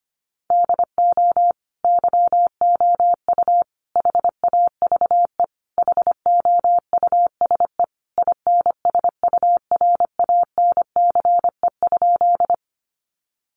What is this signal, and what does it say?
Morse code, heard as a keyed tone, about 25 wpm, 700 Hz. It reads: DO YOU HAVE HOUSE INSURANCE?
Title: Sentences from Top 1100 Words 25wpm